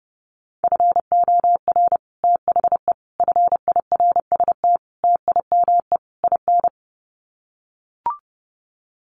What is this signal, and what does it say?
Morse code, heard as a keyed tone, about 30 wpm, 700 Hz. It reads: FOR THE FIRST TIME IN
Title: Sets of 5 Words 30wpm